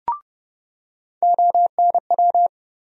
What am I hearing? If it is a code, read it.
Morse code, heard as a keyed tone, about 30 wpm, 700 Hz. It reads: ONW
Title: USA Call Sign Postfixes - 3 Letters 30wpm